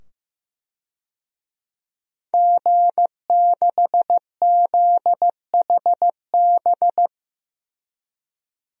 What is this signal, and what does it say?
Morse code, heard as a keyed tone, about 16 wpm, 700 Hz. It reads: G6ZHB
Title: DXCC-100 Call Signs - 5 Characters - 15wpm